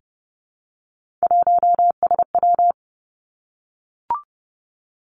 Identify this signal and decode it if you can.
Morse code, heard as a keyed tone, about 30 wpm, 700 Hz. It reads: 1SW
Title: Three Challenging Letter-Number Combinations 30wpm